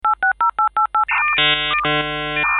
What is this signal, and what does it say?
DTMF, heard as telephone touch tones, tones of about 110 ms. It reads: 860888